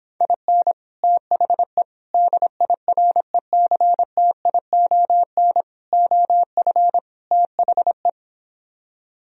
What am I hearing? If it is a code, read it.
Morse code, heard as a keyed tone, about 26 wpm, 700 Hz. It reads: IN THE DIRECTION OF THE